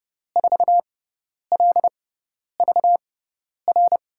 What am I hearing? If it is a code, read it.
Morse code, heard as a keyed tone, about 30 wpm, spaced as for 10 wpm, 700 Hz. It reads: 4LVR
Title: USA Call Signs - 6x Character Spacing - 30wpm